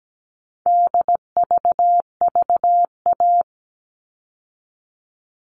Morse code, heard as a keyed tone, about 17 wpm, 700 Hz. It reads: DVVA
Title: All Cut Numbers - 4-Digit 17wpm